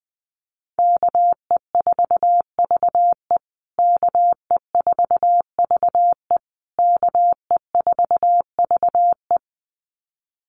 Morse code, heard as a keyed tone, about 20 wpm, 700 Hz. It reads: KE4VE KE4VE KE4VE
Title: USA Call Signs Repeated 3x 20wpm